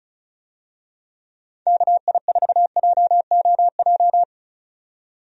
Morse code, heard as a keyed tone, about 36 wpm, 700 Hz. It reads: KI4JOJ
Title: USA Call Signs - 6 Characters - 35wpm